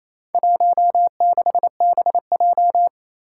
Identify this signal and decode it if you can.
Morse code, heard as a keyed tone, about 27 wpm, 700 Hz. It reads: 16BJ
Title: Four Challenging Letter-Number Combinations 28wpm